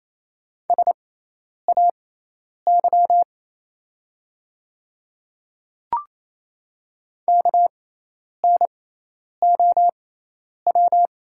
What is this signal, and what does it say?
Morse code, heard as a keyed tone, about 28 wpm, spaced as for 10 wpm, 690 Hz. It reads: SAY KNOW
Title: 6x Character Spacing - Top 200 Words - 28wpm